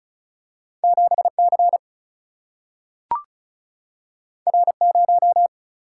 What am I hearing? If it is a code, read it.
Morse code, heard as a keyed tone, about 36 wpm, 700 Hz. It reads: 7C R0
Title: Single-Letter Single-Number 35wpm